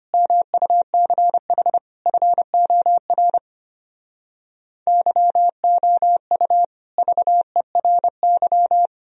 Morse code, heard as a keyed tone, about 33 wpm, 700 Hz. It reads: MUCH FOR YOU VERY